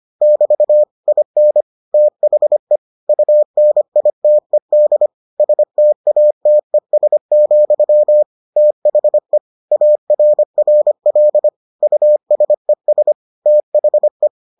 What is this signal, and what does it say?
Morse code, heard as a keyed tone, about 25 wpm, 600 Hz. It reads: <BT> IN THE UNITED STATES, THE ARRL USES THE